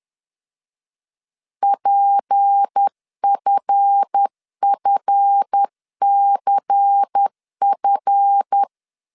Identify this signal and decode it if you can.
Morse code, heard as a keyed tone, about 10 wpm, 800 Hz. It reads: PFFCF
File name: LEZIONI / 7 GF9Z / 7 GF 10wpm 2514